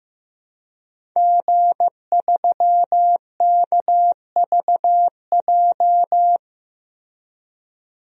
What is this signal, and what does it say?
Morse code, heard as a keyed tone, about 15 wpm, 700 Hz. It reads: G3KVJ
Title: DXCC-100 Call Signs - 5 Characters - Repeated 2x - 15wpm